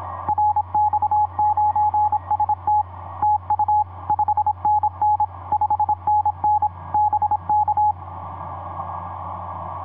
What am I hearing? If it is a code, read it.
Morse code, heard as a keyed tone, about 26 wpm, 840 Hz. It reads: RX9ST TU 5NN 5NN BK
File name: pz5ra ik5cvv etc 40m cw 14jan2010 2310